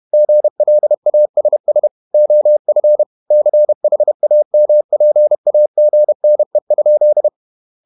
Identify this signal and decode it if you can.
Morse code, heard as a keyed tone, about 31 wpm, 600 Hz. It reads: GLASS OF CHAMPAGNE?